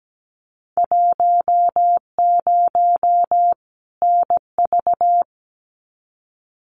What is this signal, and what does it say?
Morse code, heard as a keyed tone, about 17 wpm, 700 Hz. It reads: 10 NV
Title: Winter Field Day Exchanges 17wpm